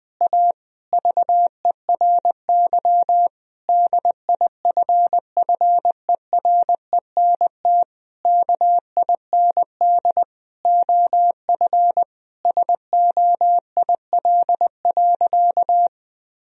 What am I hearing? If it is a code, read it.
Morse code, heard as a keyed tone, about 20 wpm, 700 Hz. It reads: A VERY DIFFERENT KIND OF SOIL.
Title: Sentences from Top 1000 Words - Review Part 4 - 20wpm